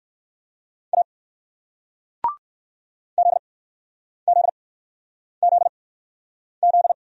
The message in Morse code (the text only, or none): A ZZZZ